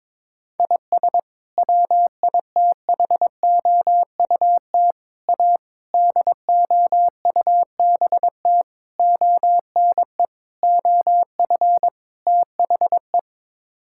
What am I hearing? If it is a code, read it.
Morse code, heard as a keyed tone, about 21 wpm, 700 Hz. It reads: IS WITHOUT A DOUBT ONE OF THE
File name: practice / 22wpm / Sets of 7 Words - 22wpm